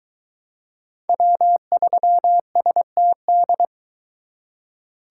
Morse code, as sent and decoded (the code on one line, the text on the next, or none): .-- ...-- ... - -..
W3STD